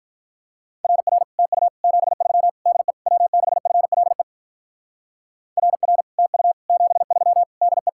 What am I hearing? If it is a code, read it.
Morse code, heard as a keyed tone, about 44 wpm, 700 Hz. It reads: RR TU 73 DE W6FLE RR TU 73 DE